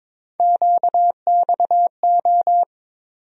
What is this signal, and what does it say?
Morse code, heard as a keyed tone, about 22 wpm, 700 Hz. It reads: QXO
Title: USA Call Sign Postfixes - 3 Letters 22wpm